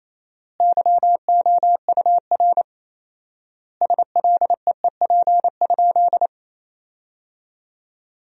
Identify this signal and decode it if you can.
Morse code, heard as a keyed tone, about 28 wpm, 700 Hz. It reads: YOUR SLEEP?
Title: 4x Word-Spacing - Sentences from Top 500 Words - 28wpm